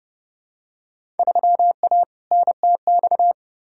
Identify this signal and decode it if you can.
Morse code, heard as a keyed tone, about 30 wpm, 700 Hz. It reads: 3A NTX
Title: Field Day Exchanges - 30wpm